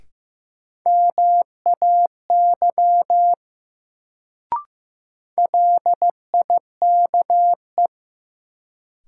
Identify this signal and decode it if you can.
Morse code, heard as a keyed tone, about 15 wpm, 700 Hz. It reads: MAY LIKE
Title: Sentences from Top 100 Words - New Word Drill - 15wpm